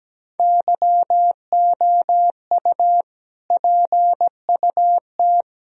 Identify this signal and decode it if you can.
Morse code, heard as a keyed tone, about 17 wpm, 700 Hz. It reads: YOU PUT
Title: Sentences from Top 1000 Words - Review Part 1 - 17wpm